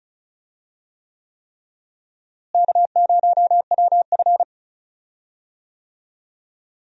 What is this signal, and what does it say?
Morse code, heard as a keyed tone, about 36 wpm, 700 Hz. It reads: K0WF
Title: USA Call Signs - 4 Characters - Repeated 3x - 35wpm